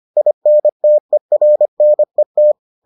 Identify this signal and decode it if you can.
Morse code, heard as a keyed tone, about 25 wpm, 600 Hz. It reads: INTERNET